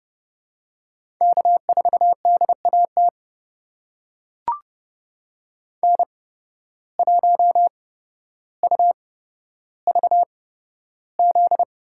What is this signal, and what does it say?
Morse code, heard as a keyed tone, about 30 wpm, spaced as for 13 wpm, 700 Hz. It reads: K4DAT N1UVZ
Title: USA Call Signs - 5 Characters - 8x Character Spacing - 30wpm